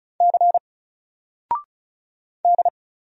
Morse code, heard as a keyed tone, about 35 wpm, 700 Hz. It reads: C D
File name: Single-Letter-Number-35wpm